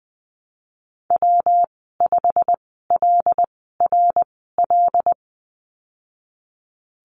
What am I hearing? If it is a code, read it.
Morse code, heard as a keyed tone, about 20 wpm, spaced as for 15 wpm, 700 Hz. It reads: W5LRL
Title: 186 Lesson 30g - Letter b - Call Signs 20wpm